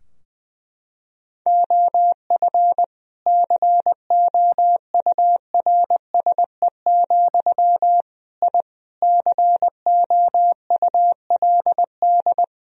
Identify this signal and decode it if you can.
Morse code, heard as a keyed tone, about 21 wpm, 700 Hz. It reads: OF COURSE, I COULD
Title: Sentences from Top 1000 Words - Review Part 3 - 20wpm